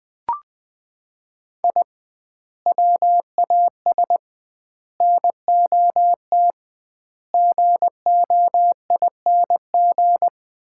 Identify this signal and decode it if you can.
Morse code, heard as a keyed tone, about 21 wpm, 700 Hz. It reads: I WAS NOT GOING
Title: 2x Word-Spacing - Sets of 5 Words - 20wpm